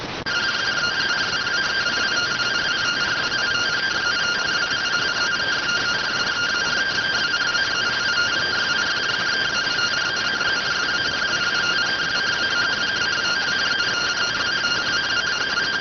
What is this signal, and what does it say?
Two-tone teleprinter signal, about 170 Hz shift.